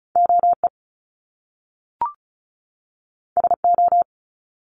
Morse code, heard as a keyed tone, about 36 wpm, 700 Hz. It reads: OE SO
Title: Top Two Letter Combinations in Words 35wpm